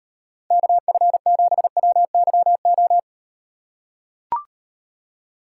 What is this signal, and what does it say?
Morse code, heard as a keyed tone, about 38 wpm, 700 Hz. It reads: KF7WYO